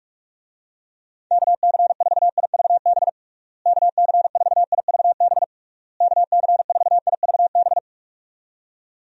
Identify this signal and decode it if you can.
Morse code, heard as a keyed tone, about 41 wpm, 700 Hz. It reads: KC4IVB KC4IVB KC4IVB
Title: USA Call Signs - 6 Characters - Repeated 3x - 45wpm